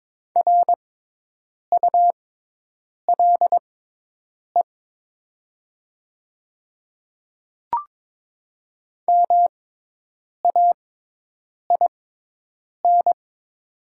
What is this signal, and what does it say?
Morse code, heard as a keyed tone, about 22 wpm, spaced as for 8 wpm, 700 Hz. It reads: RULE MAIN
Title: 6x Character Spacing - Top 500 Words - 22wpm